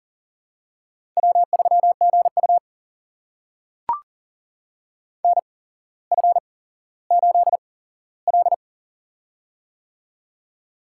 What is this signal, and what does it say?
Morse code, heard as a keyed tone, about 43 wpm, spaced as for 17 wpm, 700 Hz. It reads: W3GU NF8L
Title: USA Call Signs - 4 Characters - 8x Character Spacing - 40wpm